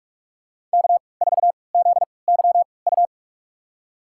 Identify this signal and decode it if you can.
Morse code, heard as a keyed tone, about 46 wpm, spaced as for 25 wpm, 700 Hz. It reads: K4ZYU
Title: USA Call Signs - 5 Characters - 3x Character Spacing - 45wpm